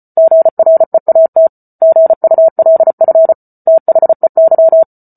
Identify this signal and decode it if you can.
Morse code, heard as a keyed tone, about 34 wpm, 650 Hz. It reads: GREAT GULF THEY